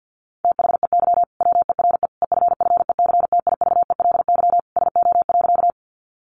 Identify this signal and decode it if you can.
Morse code, heard as a keyed tone, about 50 wpm, 700 Hz. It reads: THEY WERE EFFECTIVELY SO.
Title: Sentences from Top 1400 Words 50wpm